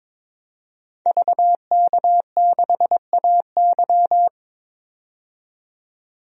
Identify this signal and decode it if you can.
Morse code, heard as a keyed tone, about 22 wpm, 700 Hz. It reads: VK6AY
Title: DXCC-100 Call Signs - 5 Characters - 22wpm